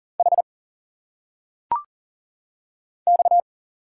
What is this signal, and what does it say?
Morse code, heard as a keyed tone, about 40 wpm, 700 Hz. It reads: H X
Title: Single Letters - Phonetic Pronunciation 40wpm